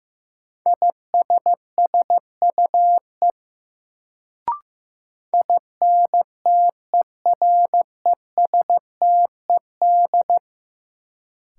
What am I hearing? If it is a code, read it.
Morse code, heard as a keyed tone, about 15 wpm, 700 Hz. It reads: ISSUE INTERESTED
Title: Sentences from Top 900 Words - New Word Drill - 15wpm